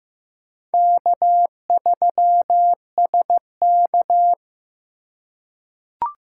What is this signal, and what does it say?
Morse code, heard as a keyed tone, about 15 wpm, 700 Hz. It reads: K3SK
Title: USA Call Signs - 4 Characters - 4x Character Spacing - 15wpm